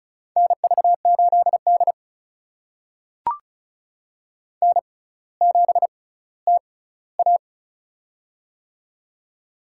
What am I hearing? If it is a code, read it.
Morse code, heard as a keyed tone, about 36 wpm, spaced as for 18 wpm, 700 Hz. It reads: NV8D N7TA
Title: USA Call Signs - 4 Characters - 6x Character Spacing - 35wpm